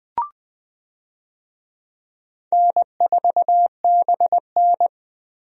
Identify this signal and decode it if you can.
Morse code, heard as a keyed tone, about 20 wpm, 700 Hz. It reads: N4BN
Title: USA Call Signs - 4 Characters - 20wpm